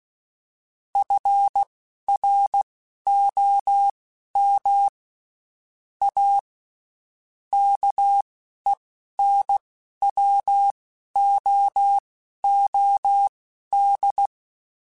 Morse code, heard as a keyed tone, about 16 wpm, spaced as for 12 wpm, 800 Hz. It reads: FROM A KENWOOD
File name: (spaced) / QSOs Spaced-mp3s / kc7sac @15 sp